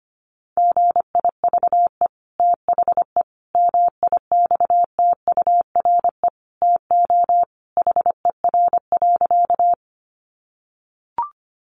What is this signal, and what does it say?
Morse code, heard as a keyed tone, about 25 wpm, 700 Hz. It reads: GIVE THE MIXTURE TO HER.
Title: Sentences from Top 1500 Words 25wpm